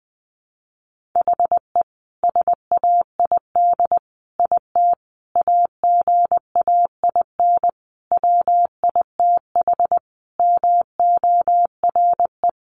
Morse code, heard as a keyed tone, about 20 wpm, 700 Hz. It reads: HE SAID IT AGAIN WITH MORE
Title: Sentences from Top 1400 Words 20wpm